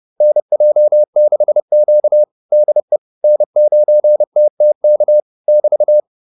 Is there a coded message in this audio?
Yes: Morse code from 0.2 to 6.0 s, heard as a keyed tone, about 30 wpm, 600 Hz. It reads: NJ6Q DE N9TTK <BT>